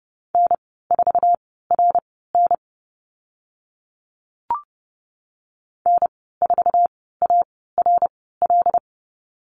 Morse code, heard as a keyed tone, about 30 wpm, spaced as for 17 wpm, 700 Hz. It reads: N4RN N4ARL
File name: Lesson 14g - Number 2 - Call Signs 30wpm